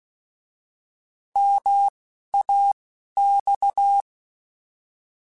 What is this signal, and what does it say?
Morse code, heard as a keyed tone, about 16 wpm, 800 Hz. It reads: M A X
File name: Contrasts / Contrasts-mp3s / 3LtrContrasts @10